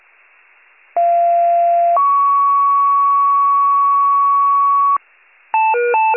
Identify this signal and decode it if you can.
Two-tone page, heard as a one-second tone, then a three-second tone, about 690 then 1120 Hz, with noise. It is heard from 1.0 to 5.0 s.
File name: pager / 41 RESCUE / wcalert